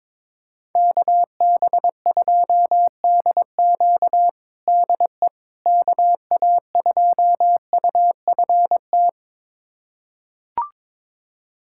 Morse code, heard as a keyed tone, about 22 wpm, 700 Hz. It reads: KB2DQ DE KA2UFT
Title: USA Callsign DE Callsign 22wpm